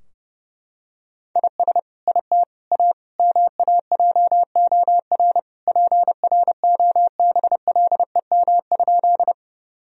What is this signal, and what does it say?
Morse code, heard as a keyed tone, about 30 wpm, 700 Hz. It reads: IS IT A MAJOR PROBLEM?
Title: Sentences from Top 1000 Words - Review Part 4 - 30wpm